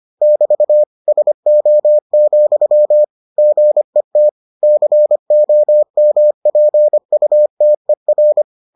Morse code, heard as a keyed tone, about 25 wpm, 600 Hz. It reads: <BT> SO, GET COMPUTER